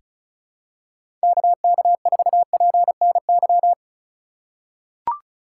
Morse code, heard as a keyed tone, about 34 wpm, 700 Hz. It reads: KK4PNY
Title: USA Call Signs - 6 Characters - 2x Character Spacing - 35wpm